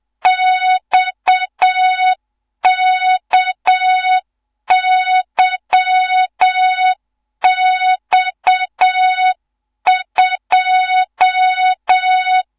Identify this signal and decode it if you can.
Morse code, heard as a keyed tone, about 7 wpm, 3700 Hz, with noise. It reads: XKYX2
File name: Morze-HA2MN / hang / 15 GYAKORLAT